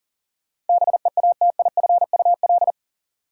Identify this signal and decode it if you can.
Morse code, heard as a keyed tone, about 40 wpm, 700 Hz. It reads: BEATIFUL
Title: Sentences from Top 1300 Words - New Word Drill Alt - 40wpm